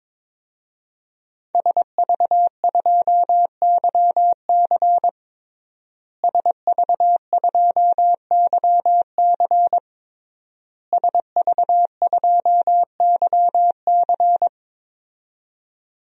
Morse code, heard as a keyed tone, about 22 wpm, 700 Hz. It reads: SV2YC SV2YC SV2YC
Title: DXCC-100 Call Signs - 5 Characters - Repeated 3x - 22wpm